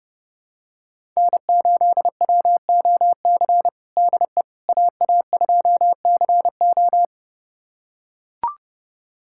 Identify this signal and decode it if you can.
Morse code, heard as a keyed tone, about 30 wpm, 700 Hz. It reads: N8WOC DE AA2CO
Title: USA Callsign DE Callsign - 5 Characters - 30wpm